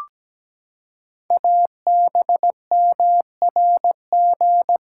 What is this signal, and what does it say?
Morse code, heard as a keyed tone, about 17 wpm, 700 Hz. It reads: ABMRG
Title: Code Groups - 5 Letters - 17wpm